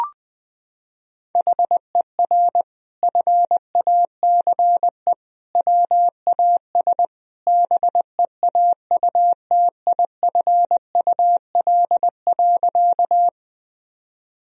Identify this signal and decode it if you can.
Morse code, heard as a keyed tone, about 20 wpm, 700 Hz. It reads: HER FACE WAS BEAUTIFUL.
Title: Sentences from Top 1000 Words - Review Part 2 - 20wpm